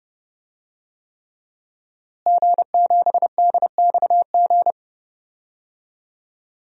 Morse code, heard as a keyed tone, about 31 wpm, 700 Hz. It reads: G7DXG
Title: DXCC-100 Call Signs - 5 Characters - Repeated 3x - 30wpm